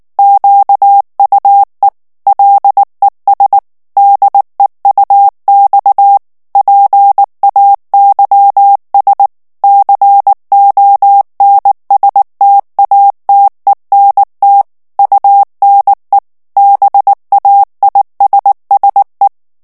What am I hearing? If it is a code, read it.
Morse code, heard as a keyed tone, about 19 wpm, 800 Hz. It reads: QUE LES DEUX PAYS CONSTATENT UNE BAISSE
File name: FRANCAIS TEXTE CLAIR 1 VITESSE 18 MOTS MINUTES